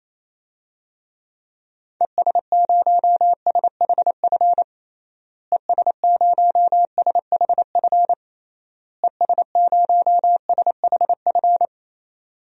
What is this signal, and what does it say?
Morse code, heard as a keyed tone, about 28 wpm, 700 Hz. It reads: ES0SHF ES0SHF ES0SHF
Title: DXCC-100 Call Signs - 6 Characters - Repeated 3x - 28wpm